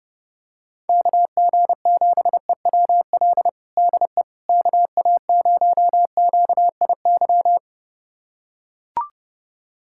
Morse code, heard as a keyed tone, about 29 wpm, 700 Hz. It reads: KG7EWL DE KA0QIY
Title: USA Callsign DE Callsign - 6 Characters - 30wpm